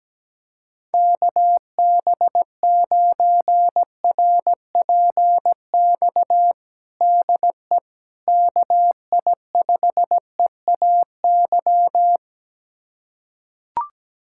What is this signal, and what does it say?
Morse code, heard as a keyed tone, about 17 wpm, 700 Hz. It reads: KB9RPX DE KI5EAY
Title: USA Callsign DE Callsign - 6 Characters - 17wpm